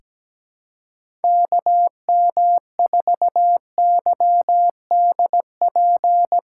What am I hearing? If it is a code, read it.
Morse code, heard as a keyed tone, about 17 wpm, 700 Hz. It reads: KM4YDP
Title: USA Call Signs - 6 Characters - 3x Character Spacing - 17wpm